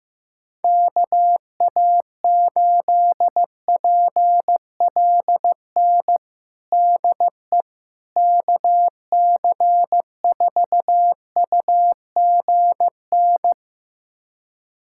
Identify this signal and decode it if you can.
Morse code, heard as a keyed tone, about 16 wpm, 700 Hz. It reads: KA8PLN DE KC4UGN
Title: USA Callsign DE Callsign 15wpm